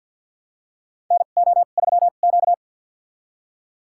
Morse code, heard as a keyed tone, about 51 wpm, spaced as for 36 wpm, 700 Hz. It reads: NO3Q
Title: USA Call Signs - 4 Characters - 2x Character Spacing - 50wpm